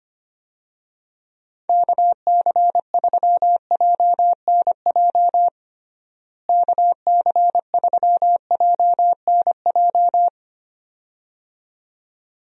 Morse code, heard as a keyed tone, about 25 wpm, 700 Hz. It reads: KC3JNJ KC3JNJ
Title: USA Call Signs - 6 Characters - Repeated 2x - 25wpm